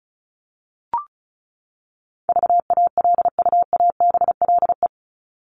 Morse code, heard as a keyed tone, about 34 wpm, 700 Hz. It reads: VALUABLE
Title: Top 1001-1250 Words 35wpm